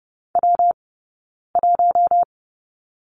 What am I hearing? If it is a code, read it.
Morse code, heard as a keyed tone, about 30 wpm, spaced as for 9 wpm, 690 Hz. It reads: W1
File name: Lesson 19d - Letter w - Two Characters 30wpm